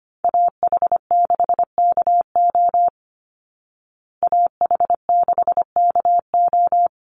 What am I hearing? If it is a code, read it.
Morse code, heard as a keyed tone, about 25 wpm, 700 Hz. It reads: AH6KO AH6KO
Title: Call Signs Repeated 3x - 25wpm